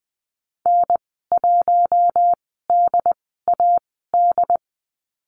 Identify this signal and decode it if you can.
Morse code, heard as a keyed tone, about 20 wpm, spaced as for 14 wpm, 700 Hz. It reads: N1DAD
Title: 069 Lesson 12g - Letter d - Call Signs 20wpm